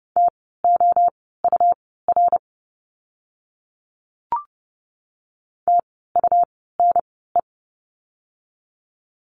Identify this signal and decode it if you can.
Morse code, heard as a keyed tone, about 30 wpm, spaced as for 17 wpm, 700 Hz. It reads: TOUR TUNE
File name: Lesson 17f - Letter u - Words 30wpm